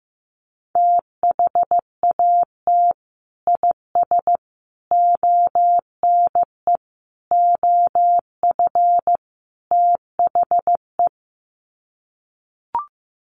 Morse code, heard as a keyed tone, about 15 wpm, 700 Hz. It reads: THAT IS ONE OF THE
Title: Sets of 5 Words Encore 15wpm